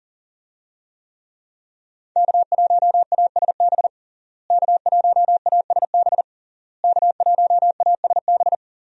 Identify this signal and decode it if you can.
Morse code, heard as a keyed tone, about 40 wpm, 700 Hz. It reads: K1ASB K1ASB K1ASB
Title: USA Call Signs - 5 Characters - Repeated 3x - 40wpm